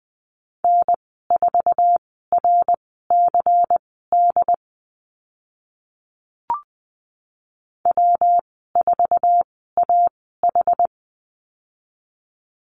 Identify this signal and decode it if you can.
Morse code, heard as a keyed tone, about 20 wpm, spaced as for 14 wpm, 700 Hz. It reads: N4RCD W4AH
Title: 122 Lesson 20g - Number 3 - Call Signs 20wpm